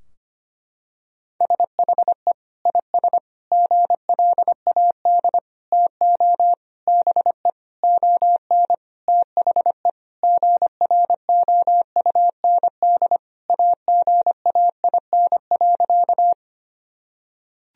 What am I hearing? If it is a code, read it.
Morse code, heard as a keyed tone, about 25 wpm, 700 Hz. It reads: SHE IS GLAD TO BE ON THE GROUND AGAIN.
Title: Sentences from Top 1000 Words - Review Part 2 - 25wpm